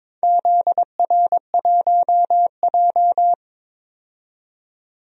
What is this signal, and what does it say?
Morse code, heard as a keyed tone, about 21 wpm, 700 Hz. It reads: ZR1J
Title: DXCC-100 Call Signs - 4 Characters - 22wpm